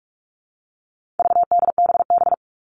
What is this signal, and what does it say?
Morse code, heard as a keyed tone, about 44 wpm, 700 Hz. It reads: VDBB
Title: All Cut Numbers - 4-Digit 45wpm